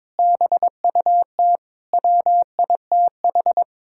Morse code, heard as a keyed tone, about 22 wpm, 700 Hz. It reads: BUT WITH
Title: Sentences from Top 1000 Words - Review Part 5 - 22wpm